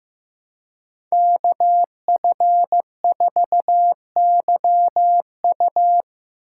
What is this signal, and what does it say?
Morse code, heard as a keyed tone, about 18 wpm, 700 Hz. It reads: KF4YU
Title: Speed Racing - USA 2x2 Call Signs 15wpm